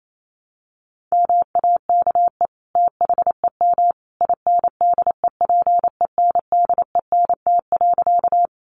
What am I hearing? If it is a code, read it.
Morse code, heard as a keyed tone, about 28 wpm, 700 Hz. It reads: MAKE THEM INDEPENDENT.
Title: Sentences from Top 900 Words 28wpm